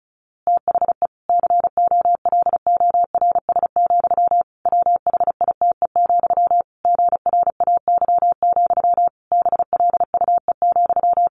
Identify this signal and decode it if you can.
Morse code, heard as a keyed tone, about 34 wpm, 700 Hz. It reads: THE COLORS, WHITE, GRAY, BLUE,
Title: A Sound Of Thunder 35wpm